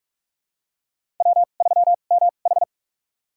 Morse code, heard as a keyed tone, about 47 wpm, spaced as for 33 wpm, 700 Hz. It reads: W3MH
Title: USA Call Signs - 4 Characters - 2x Character Spacing - 45wpm